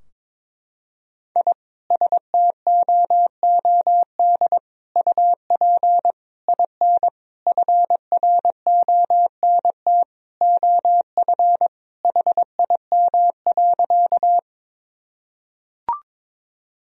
Morse code, heard as a keyed tone, about 22 wpm, 700 Hz. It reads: I STOOD UP IN FRONT OF HIM.
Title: Sentences from Top 1000 Words - Review Part 2 - 22wpm